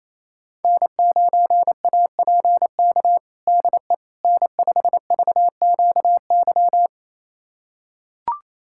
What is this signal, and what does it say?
Morse code, heard as a keyed tone, about 28 wpm, 690 Hz. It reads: N9APK DE N5VQY
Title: USA Callsign DE Callsign - 5 Characters - 28wpm